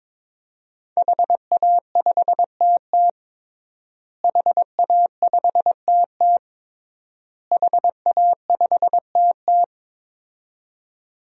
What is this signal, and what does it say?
Morse code, heard as a keyed tone, about 22 wpm, 700 Hz. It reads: HA5TT HA5TT HA5TT
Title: DXCC-100 Call Signs - 5 Characters - Repeated 3x - 22wpm